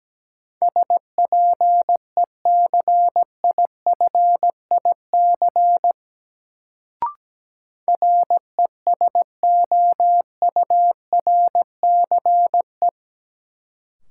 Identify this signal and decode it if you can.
Morse code, heard as a keyed tone, about 17 wpm, 700 Hz. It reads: SPECIFIC RESOURCE
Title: Sentences from Top 1300 Words - New Word Drill Alt - 17wpm